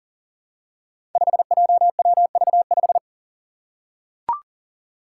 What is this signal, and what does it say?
Morse code, heard as a keyed tone, about 40 wpm, 690 Hz. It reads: 5JWV5
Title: Five Challenging Letter-Number Combinations 40wpm